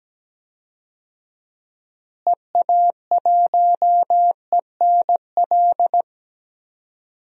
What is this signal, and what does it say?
Morse code, heard as a keyed tone, about 17 wpm, 700 Hz. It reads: EA1ENL
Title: DXCC-100 Call Signs - 6 Characters - Repeated 3x - 17wpm